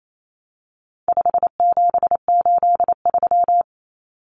Morse code, heard as a keyed tone, about 28 wpm, 700 Hz. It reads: 5783
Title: Numbers - 4-Digit 28wpm